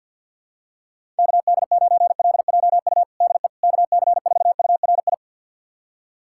Morse code, heard as a keyed tone, about 46 wpm, 700 Hz. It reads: KD9LJU DE KC4URI
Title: USA Callsign DE Callsign 50wpm